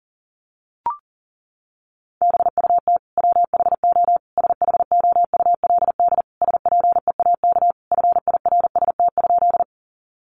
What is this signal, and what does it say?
Morse code, heard as a keyed tone, about 37 wpm, 700 Hz. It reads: BUT WHO SHOULD SPEAK FIRST?
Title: Sentences from Top 1100 Words 40wpm